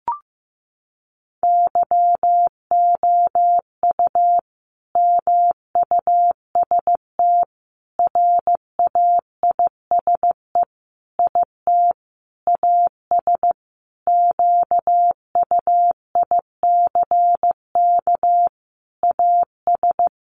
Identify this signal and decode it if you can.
Morse code, heard as a keyed tone, about 15 wpm, 700 Hz. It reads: YOU MUST RAISE IT AS QUICK AS
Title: Sentences from Top 1400 Words 15wpm